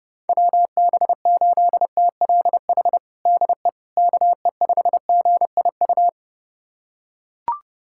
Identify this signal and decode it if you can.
Morse code, heard as a keyed tone, about 32 wpm, 700 Hz. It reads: WB8TLH DE KE5GIU
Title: USA Callsign DE Callsign - 6 Characters - 30wpm